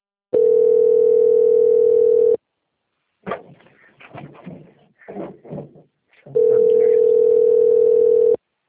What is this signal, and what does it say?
Ringback tone (North America), 440 + 480 Hz.